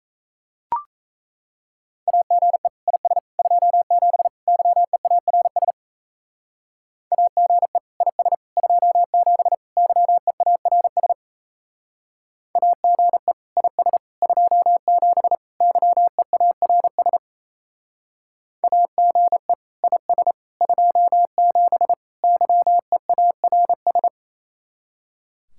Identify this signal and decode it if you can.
Morse code, heard as a keyed tone, about 33 wpm, 700 Hz. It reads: AGE IS 27 YEARS AGE IS 27 YEARS AGE IS 27 YEARS AGE IS 27 YEARS
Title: Speed-Racing Common QSO Phrases - 28wpm